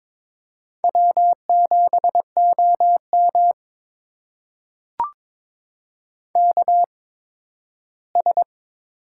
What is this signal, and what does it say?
Morse code, heard as a keyed tone, about 22 wpm, spaced as for 13 wpm, 700 Hz. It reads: W7OM KS
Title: USA Call Signs - 4 Characters - 8x Character Spacing - 22wpm